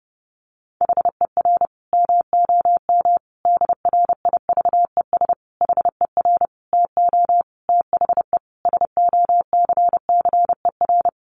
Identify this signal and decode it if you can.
Morse code, heard as a keyed tone, about 30 wpm, 700 Hz. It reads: HER MOM DRIVES HER TO THE SOCCER
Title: Easy Phrases 30wpm